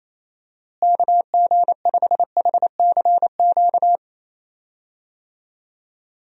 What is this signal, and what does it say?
Morse code, heard as a keyed tone, about 28 wpm, 700 Hz. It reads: KG5HCQ
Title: USA Call Signs - 6 Characters - Repeated 2x - 28wpm